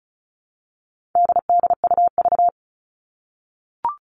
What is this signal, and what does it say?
Morse code, heard as a keyed tone, about 34 wpm, 700 Hz. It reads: DDUV
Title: All Cut Numbers - 4-Digit 35wpm